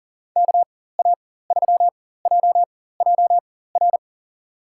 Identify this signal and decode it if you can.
Morse code, heard as a keyed tone, about 40 wpm, spaced as for 19 wpm, 700 Hz. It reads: KA3JJR
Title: USA Call Signs - 6 Characters - 4x Character Spacing - 40wpm